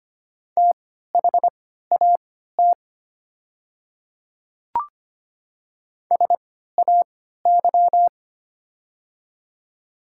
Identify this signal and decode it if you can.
Morse code, heard as a keyed tone, about 25 wpm, spaced as for 14 wpm, 700 Hz. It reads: THAT SAY
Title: 3x Character Spacing - Top 200 Words - 25wpm